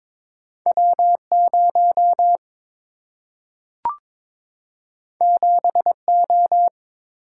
Morse code, heard as a keyed tone, about 22 wpm, 700 Hz. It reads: W0 7O
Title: Single-Letter Single-Number 22wpm